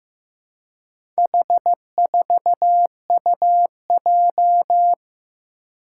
Morse code, heard as a keyed tone, about 16 wpm, 700 Hz. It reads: H4UJ
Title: Four Challenging Letter-Number Combinations 15wpm